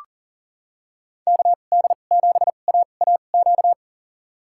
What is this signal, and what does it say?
Morse code, heard as a keyed tone, about 40 wpm, spaced as for 29 wpm, 700 Hz. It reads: KD7AAQ
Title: USA Call Signs - 6 Characters - 2x Character Spacing - 40wpm